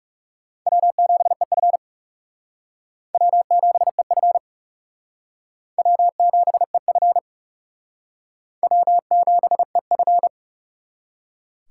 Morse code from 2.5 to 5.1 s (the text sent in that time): W7EF